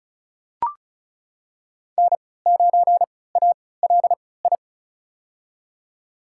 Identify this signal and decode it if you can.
Morse code, heard as a keyed tone, about 36 wpm, spaced as for 20 wpm, 700 Hz. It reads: N9ALI
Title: USA Call Signs - 5 Characters - 3x Character Spacing - 35wpm